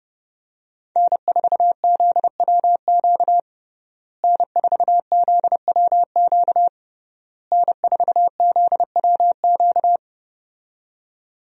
Morse code, heard as a keyed tone, about 32 wpm, 700 Hz. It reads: N4ZWQ N4ZWQ N4ZWQ
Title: USA Call Signs - 5 Characters - Repeated 3x - 30wpm